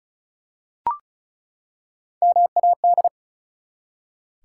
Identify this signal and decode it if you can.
Morse code, heard as a keyed tone, about 34 wpm, 700 Hz. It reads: MAD
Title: Three Letter Words - 35wpm